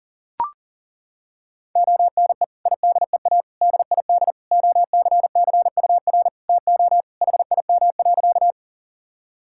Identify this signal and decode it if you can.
Morse code, heard as a keyed tone, about 40 wpm, 700 Hz. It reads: ONE IDEA DID OCCUR TO HIM.
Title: Sentences from Top 1000 Words - Review Part 6 - 40wpm